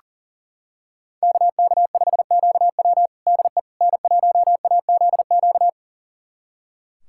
Morse code, heard as a keyed tone, about 40 wpm, 700 Hz. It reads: KK5QW DE N1AZQ
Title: USA Callsign DE Callsign - 5 Characters - 40wpm